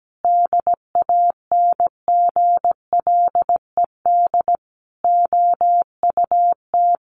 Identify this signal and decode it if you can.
Morse code, heard as a keyed tone, about 17 wpm, 700 Hz. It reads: DANGLED OUT
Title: A Sound Of Thunder 17wpm